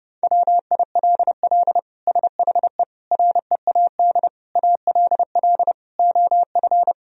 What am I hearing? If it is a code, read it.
Morse code, heard as a keyed tone, about 30 wpm, 700 Hz. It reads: WILL SHE READ ALL OF
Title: Sentences from Top 1000 Words - Review Part 4 - 30wpm